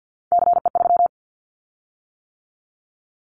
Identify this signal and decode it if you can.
Morse code, heard as a keyed tone, about 50 wpm, 700 Hz. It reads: CE3
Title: Callsign Prefixes for Top 100 DXCC - 50wpm